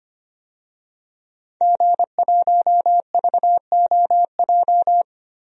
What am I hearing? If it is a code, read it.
Morse code, heard as a keyed tone, about 25 wpm, 700 Hz. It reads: G1VOJ
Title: DXCC-100 Call Signs - 5 Characters - Repeated 3x - 25wpm